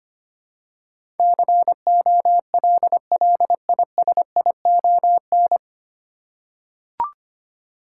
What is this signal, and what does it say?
Morse code, heard as a keyed tone, about 25 wpm, 700 Hz. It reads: COLLISION
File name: Nine Letter Words - 25wpm